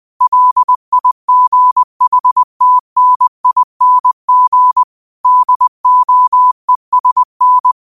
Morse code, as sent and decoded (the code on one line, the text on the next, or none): .-.. .. --. .... - -. .. -. --. / -.. --- . ... -.
LIGHTNING DOESN